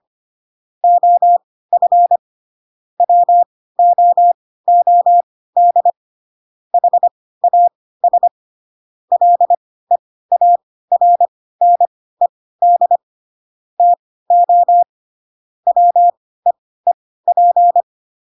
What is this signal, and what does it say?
Morse code, heard as a keyed tone, about 25 wpm, spaced as for 16 wpm, 700 Hz. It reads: OF WOOD HAS LEARNED TO WEEP
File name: Pinocchio / Pinocchio ch1 10 25 ik1hge